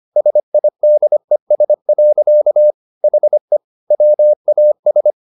Morse code, heard as a keyed tone, about 25 wpm, 600 Hz. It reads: SIDES. HE WAS
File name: the old man and the sea part 1 25wpm